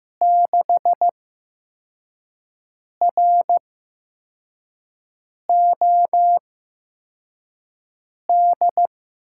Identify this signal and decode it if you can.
Morse code, heard as a keyed tone, about 15 wpm, spaced as for 4 wpm, 700 Hz. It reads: 6ROD
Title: USA Call Signs - 6 Characters - 8x Character Spacing - 15wpm